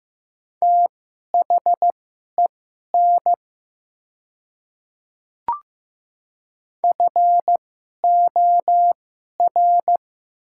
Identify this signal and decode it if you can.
Morse code, heard as a keyed tone, about 15 wpm, spaced as for 11 wpm, 700 Hz. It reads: THEN FOR
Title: 2x Character Spacing - Top 100 Words - 15wpm